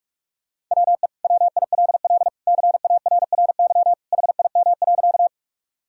Morse code, heard as a keyed tone, about 46 wpm, 700 Hz. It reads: WE WILL CARRY HIM.
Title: Sentences from Top 1000 Words - Review Part 1 - 45wpm